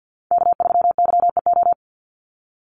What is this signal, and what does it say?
Morse code, heard as a keyed tone, about 50 wpm, 700 Hz. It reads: K3YEO
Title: Call Signs - 50wpm